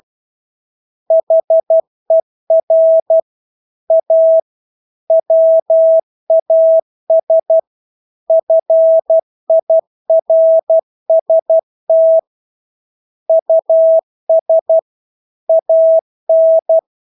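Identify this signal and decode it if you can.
Morse code, heard as a keyed tone, about 12 wpm, 650 Hz. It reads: HER A WAS FIRST US AN
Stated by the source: Morse practice of English words at 12 wpm.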